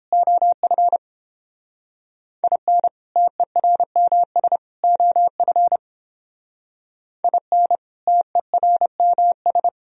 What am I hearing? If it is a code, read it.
Morse code, heard as a keyed tone, about 31 wpm, 700 Hz. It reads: OF IN TERMS OF IN TERMS